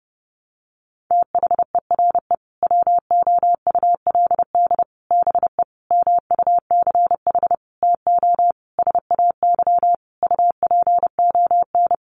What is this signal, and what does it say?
Morse code, heard as a keyed tone, about 30 wpm, 700 Hz. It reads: THERE WOULD BE MUCH TO SAY UPON
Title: Sentences from Top 1200 Words 30wpm